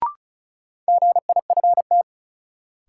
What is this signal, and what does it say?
Morse code, heard as a keyed tone, about 36 wpm, 700 Hz. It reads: GIFT
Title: Sentences from Top 900 Words - New Word Drill - 35wpm